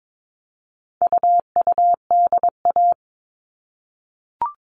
Morse code, heard as a keyed tone, about 22 wpm, 700 Hz. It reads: UUDA
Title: All Cut Numbers - 4-Digit 22wpm